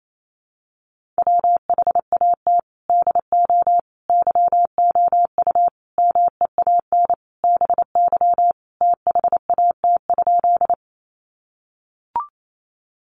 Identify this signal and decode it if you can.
Morse code, heard as a keyed tone, about 28 wpm, 700 Hz. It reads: WHAT DO YOU MEAN BY THAT?
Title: Sentences from Top 200 Words 28wpm